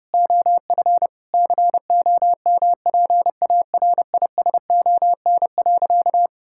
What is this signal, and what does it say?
Morse code, heard as a keyed tone, about 30 wpm, 700 Hz. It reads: OF COMPARISON.
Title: Sentences from Top 1600 Words 30wpm